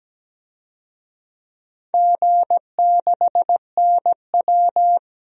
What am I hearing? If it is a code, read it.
Morse code, heard as a keyed tone, about 17 wpm, 700 Hz. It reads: G6NW
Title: DXCC-100 Call Signs - 4 Characters - 17wpm